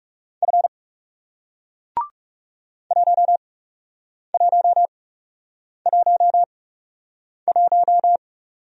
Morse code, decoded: F 1111

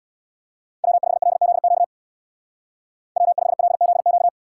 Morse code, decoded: W5LZQ W5LZQ

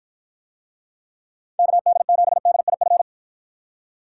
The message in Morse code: KD7DIF

